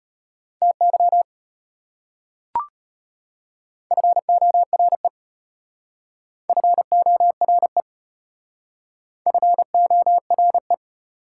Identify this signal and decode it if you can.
Morse code, heard as a keyed tone, about 32 wpm, 700 Hz. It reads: TY FORE FORE FORE